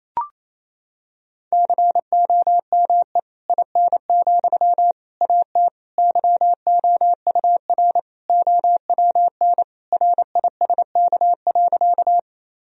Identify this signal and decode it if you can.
Morse code, heard as a keyed tone, about 27 wpm, 700 Hz. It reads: COME IN, AT YOUR OWN RISK.
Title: Sentences from Top 1000 Words - Review Part 4 - 28wpm